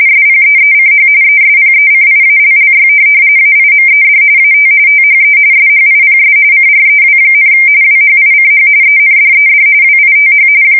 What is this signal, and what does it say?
RTTY, about 45.45 baud. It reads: SPEECH ESPECIALLY AIMED AT THE SHIPS CREW, WHICH ENDED WITH
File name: fsk170 / USS-Missouri